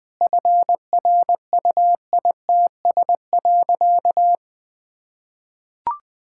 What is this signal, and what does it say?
Morse code, heard as a keyed tone, about 20 wpm, 700 Hz. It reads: FRUITS.
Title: Sentences with 7-Word Phrases - 20wpm